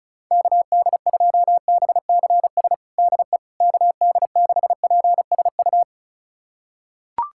Morse code, heard as a keyed tone, about 34 wpm, 700 Hz. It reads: KD2BCS DE KD6PSU
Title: USA Callsign DE Callsign 35wpm